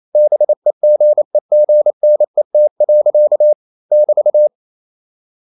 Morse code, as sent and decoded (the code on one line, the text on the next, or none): -... --. --. -. - .-.-.- / -...-
BEGEGNET. <BT>